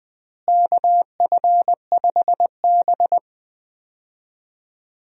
Morse code, heard as a keyed tone, about 20 wpm, 700 Hz. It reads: KF5B
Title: USA Call Signs - 4 Characters - Repeated 2x - 20wpm